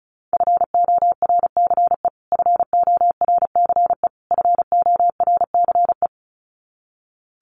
Morse code, heard as a keyed tone, about 34 wpm, 700 Hz. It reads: FORCE FORCE FORCE
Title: Top 500 Words Repeated 35wpm